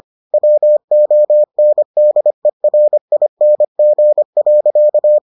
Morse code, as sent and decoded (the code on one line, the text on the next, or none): .-- --- -. -.. .-. .. -. --. .-.-.-
WONDERING.